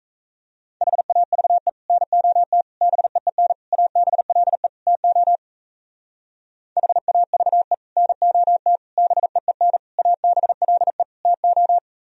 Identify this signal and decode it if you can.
Morse code, heard as a keyed tone, about 36 wpm, 700 Hz. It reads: HAVE NOT BEEN ABLE TO HAVE NOT BEEN ABLE TO